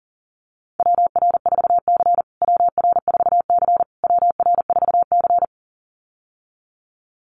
Morse code, heard as a keyed tone, about 40 wpm, 700 Hz. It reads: WR4C WR4C WR4C
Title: USA Call Signs Repeated 3x 40wpm